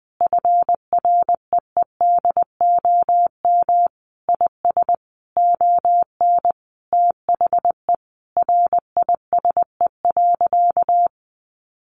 Morse code, heard as a keyed tone, about 20 wpm, 700 Hz. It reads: FREEDOM IS ON THE RISE.
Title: Sentences from Top 1300 Words 20wpm